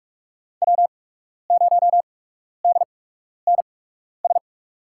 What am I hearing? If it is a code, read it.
Morse code, heard as a keyed tone, about 44 wpm, spaced as for 12 wpm, 690 Hz. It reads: W0DNS
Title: USA Call Signs - 5 Characters - 8x Character Spacing - 45wpm